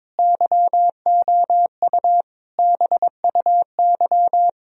Morse code, heard as a keyed tone, about 22 wpm, 700 Hz. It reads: YOU BUY
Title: Sentences from Top 1000 Words - Review Part 4 - 22wpm